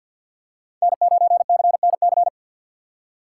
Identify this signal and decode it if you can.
Morse code, heard as a keyed tone, about 47 wpm, 690 Hz. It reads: N9CNC
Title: USA Call Signs - 5 Characters - 2x Character Spacing - 50wpm